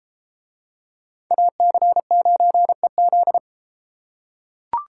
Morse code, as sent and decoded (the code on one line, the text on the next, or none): .- -.-. ----. --..
AC9EZ